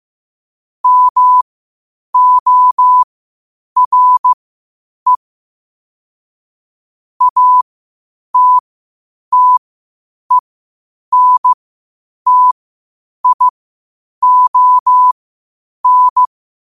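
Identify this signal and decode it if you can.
Morse code, heard as a keyed tone, about 15 wpm, spaced as for 8 wpm, 1000 Hz. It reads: MORE ATTENTION
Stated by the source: Morse code quotes at 8 words per minute from QOTD at 8 WPM.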